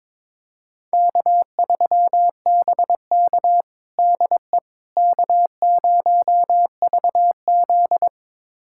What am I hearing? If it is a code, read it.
Morse code, heard as a keyed tone, about 22 wpm, 700 Hz. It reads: K3BK DE K0VZ
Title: USA Callsign DE Callsign - 4 Characters - 22wpm